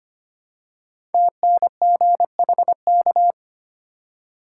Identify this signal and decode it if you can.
Morse code, heard as a keyed tone, about 25 wpm, 700 Hz. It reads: TNGHK
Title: Code Groups - 5 Letter:Numbers - 25wpm